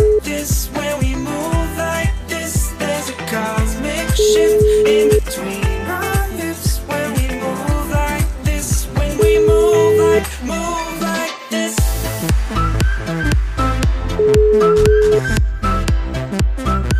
Ringback tone (Europe, 425 Hz), with noise.